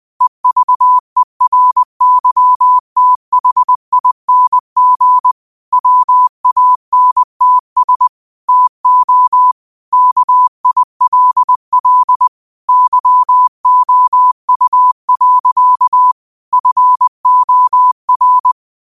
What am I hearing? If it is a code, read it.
Morse code, heard as a keyed tone, about 20 wpm, 1000 Hz. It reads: EVERYTHING WANTS TO KILL YOU. FOR